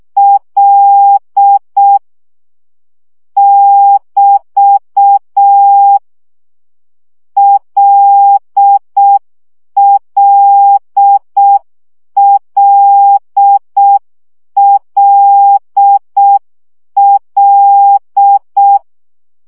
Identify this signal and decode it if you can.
Morse code, heard as a keyed tone, about 6 wpm, 800 Hz. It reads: L <BT> LLLLL
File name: CW gyakorlas l 5 baud